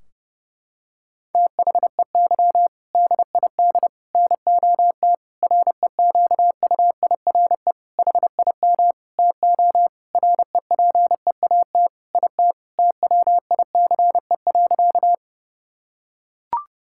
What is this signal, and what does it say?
Morse code, heard as a keyed tone, about 31 wpm, 700 Hz. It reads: THEY DID NOT REQUIRE HIM TO REPEAT IT TWICE.
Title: Sentences from Top 1600 Words 30wpm